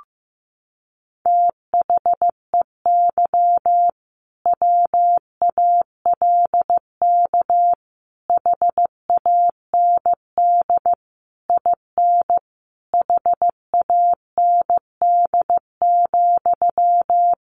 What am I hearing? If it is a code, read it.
Morse code, heard as a keyed tone, about 15 wpm, 700 Hz. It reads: THEY WALK HAND IN HAND,
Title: Sentences from Top 1400 Words 15wpm